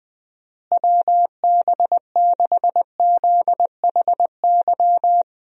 Morse code, heard as a keyed tone, about 20 wpm, 700 Hz. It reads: WB6ZHY
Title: USA Call Signs - 6 Characters - Repeated 3x - 20wpm